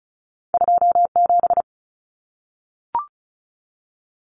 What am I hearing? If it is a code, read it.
Morse code, heard as a keyed tone, about 36 wpm, 700 Hz. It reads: 27
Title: Common Cut Numbers - 2-Digit 35wpm